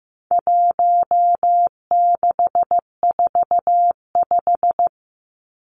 Morse code, heard as a keyed tone, about 15 wpm, 700 Hz. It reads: 1645
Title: Numbers - 4-Digit 15wpm